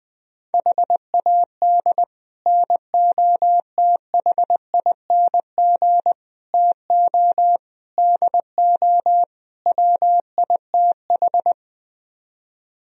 Morse code, heard as a keyed tone, about 20 wpm, 700 Hz. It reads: HAD NOTHING TO DO WITH